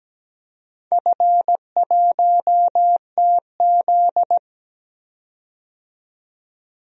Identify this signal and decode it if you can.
Morse code, heard as a keyed tone, about 17 wpm, 700 Hz. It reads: F1TZ
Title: DXCC-100 Call Signs - 4 Characters - 17wpm